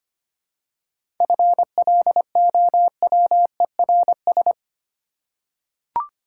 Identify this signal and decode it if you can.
Morse code, heard as a keyed tone, about 25 wpm, 700 Hz. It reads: FLOWERS
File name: practice / 25wpm / Easy Words with Hints - 25wpm